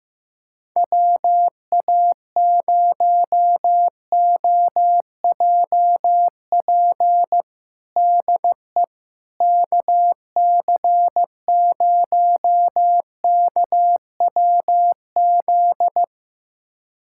Morse code, heard as a keyed tone, about 15 wpm, 700 Hz. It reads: WA0OJP DE KC0KWZ
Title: USA Callsign DE Callsign - 6 Characters - 15wpm